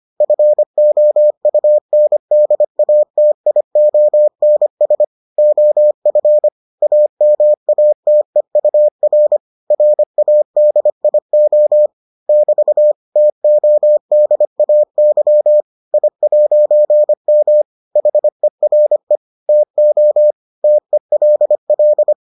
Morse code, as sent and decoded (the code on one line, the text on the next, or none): ..-. --- ..- -. -.. .- - .. --- -. ... / --- ..-. / .- -- .- - . ..- .-. / .-. .- -.. .. --- / -...- / - --- -.. .- -.-- / .. .----. -- / .... .-. / - --- / - . .-.. .-..
FOUNDATIONS OF AMATEUR RADIO <BT> TODAY I'M HERE TO TELL